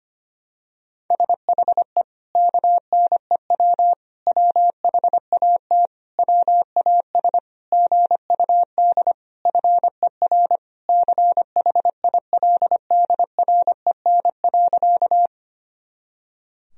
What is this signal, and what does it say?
Morse code, heard as a keyed tone, about 25 wpm, 700 Hz. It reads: SHE KNEW WHAT WAS GUD FER CHILDREN.
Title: Sentences from Top 400 Words and Abbreviations 25wpm